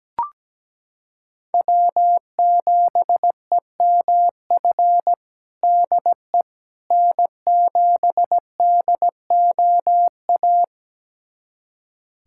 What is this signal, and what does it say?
Morse code, heard as a keyed tone, about 17 wpm, 700 Hz. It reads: W7EMF DE N7DOA
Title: USA Callsign DE Callsign - 5 Characters - 17wpm